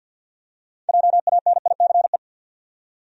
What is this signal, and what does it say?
Morse code, heard as a keyed tone, about 52 wpm, 700 Hz. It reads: JANICE
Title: USA Common Names - 50wpm